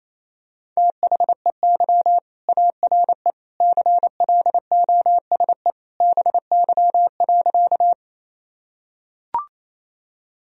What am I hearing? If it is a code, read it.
Morse code, heard as a keyed tone, about 28 wpm, 700 Hz. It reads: THEY ARE CLOSE BY.
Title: Sentences from Top 1000 Words - Review Part 1 - 28wpm